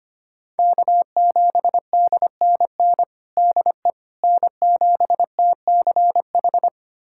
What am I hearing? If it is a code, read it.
Morse code, heard as a keyed tone, about 25 wpm, 700 Hz. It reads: K7DNN DE N7TCH
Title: USA Callsign DE Callsign - 5 Characters - 25wpm